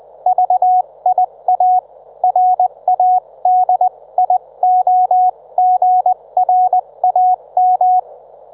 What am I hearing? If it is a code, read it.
Morse code, heard as a keyed tone, about 20 wpm, spaced as for 17 wpm, 730 Hz. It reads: VIA RADIOGRAM
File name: Trivia